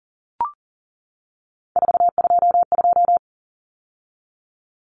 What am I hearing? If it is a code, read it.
Morse code, heard as a keyed tone, about 40 wpm, 700 Hz. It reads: 422
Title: Common Cut Numbers - 3-Digit 40wpm